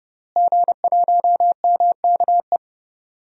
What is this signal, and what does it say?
Morse code, heard as a keyed tone, about 31 wpm, 700 Hz. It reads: G1MKE
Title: DXCC-100 Call Signs - 5 Characters - Repeated 3x - 30wpm